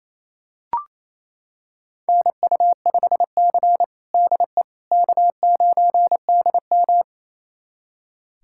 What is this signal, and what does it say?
Morse code, heard as a keyed tone, about 27 wpm, 690 Hz. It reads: NU5C DE K9DM
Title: USA Callsign DE Callsign - 4 Characters - 28wpm